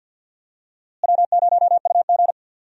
Morse code, heard as a keyed tone, about 53 wpm, 700 Hz. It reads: W0UG
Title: USA Call Signs - 4 Characters - Repeated 3x - 50wpm